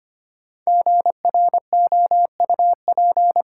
Morse code, heard as a keyed tone, about 25 wpm, 700 Hz. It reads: GROUP
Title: Sentences from Top 1000 Words - Review Part 1 - New Word Drill - 25wpm